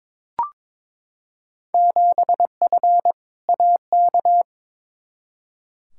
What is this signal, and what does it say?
Morse code, heard as a keyed tone, about 22 wpm, 700 Hz. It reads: 7F AK
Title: Field Day Exchanges Encore - 22wpm